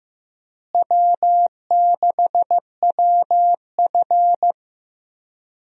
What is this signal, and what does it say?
Morse code, heard as a keyed tone, about 15 wpm, 700 Hz. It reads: W6WF
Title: USA Call Signs - 4 Characters - Repeated 2x - 15wpm